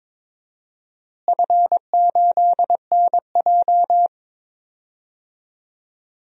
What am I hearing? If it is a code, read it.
Morse code, heard as a keyed tone, about 21 wpm, 700 Hz. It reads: F8NJ
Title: DXCC-100 Call Signs - 4 Characters - Repeated 3x - 22wpm